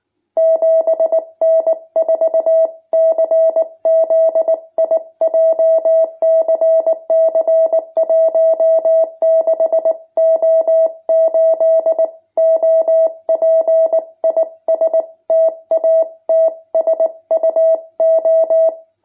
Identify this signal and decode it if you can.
Morse code, heard as a keyed tone, about 19 wpm, 650 Hz. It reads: 7N4CZI JCC 16O8 OP IS TATSUO
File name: jh8cbh / 2021cw